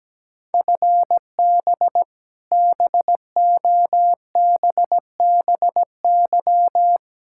Morse code, heard as a keyed tone, about 17 wpm, 700 Hz. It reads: FB BOBBY
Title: Practice QSO - Exchange 3 - More Info - 17wpm